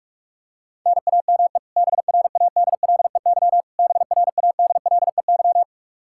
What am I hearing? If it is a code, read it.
Morse code, heard as a keyed tone, about 45 wpm, 700 Hz. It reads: NAME BRADLEY BRADLEY
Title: Practice QSO - Exchange 1 - Beginning - 45wpm